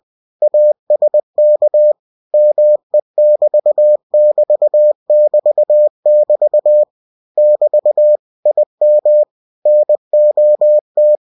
Morse code, heard as a keyed tone, about 20 wpm, 600 Hz. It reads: ASK ME<BT><BT><BT><BT> <BT> IM NOT
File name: cw-books / winnie-the-pooh / winnie-20wpm-page-0080